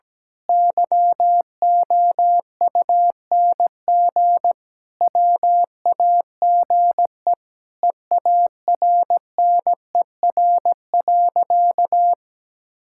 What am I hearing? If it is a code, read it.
Morse code, heard as a keyed tone, about 17 wpm, 700 Hz. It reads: YOUNG WAGE EARNER.
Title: Pangram Challenge v2 - 17wpm